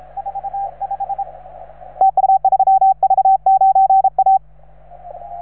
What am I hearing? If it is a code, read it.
Morse code, heard as a keyed tone, about 35 wpm, 780 Hz, with radio noise. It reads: 45 TU 3V9A